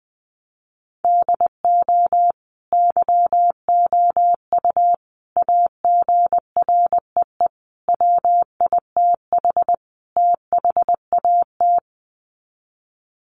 Morse code, heard as a keyed tone, about 20 wpm, 700 Hz. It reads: DO YOU AGREE WITH THAT
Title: Sets of 5 Words Encore 20wpm